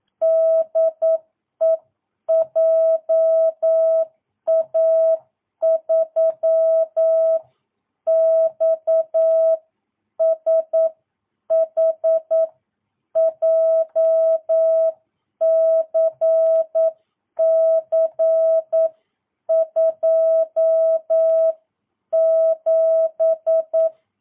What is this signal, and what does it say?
Morse code, heard as a keyed tone, about 9 wpm, spaced as for 8 wpm, 650 Hz. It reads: DE JA3XSH JCC 27
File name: jh8cbh / 2022cw